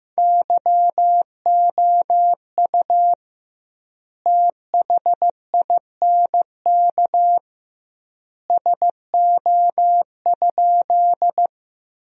Morse code, heard as a keyed tone, about 16 wpm, 700 Hz. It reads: YOU THINK SO?
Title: 2x Word-Spacing - Sentences from Top 200 Words - 15wpm